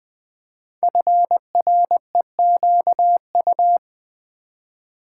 Morse code, heard as a keyed tone, about 20 wpm, 700 Hz. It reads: FREQU
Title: Build A Word - 5 Letters At A Time - 10-Letter Words - 20wpm